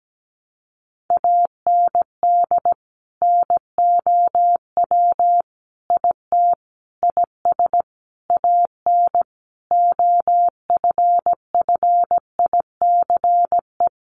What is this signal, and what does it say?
Morse code, heard as a keyed tone, about 17 wpm, 700 Hz. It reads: AND NOW IT IS AN OFFICE
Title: Sentences from Top 1000 Words 17wpm